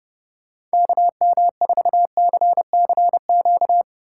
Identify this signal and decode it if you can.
Morse code, heard as a keyed tone, about 28 wpm, 700 Hz. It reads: KM4CCQ